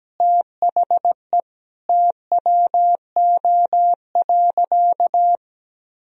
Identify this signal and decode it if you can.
Morse code, heard as a keyed tone, about 17 wpm, 700 Hz. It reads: THE TWO.
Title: Sentences from Top 1000 Words - Review Part 4 - 17wpm